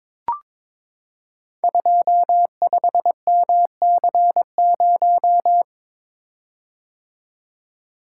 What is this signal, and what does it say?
Morse code, heard as a keyed tone, about 21 wpm, 700 Hz. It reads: 25MC0
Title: Code Groups - 5 Letter:Numbers - 22wpm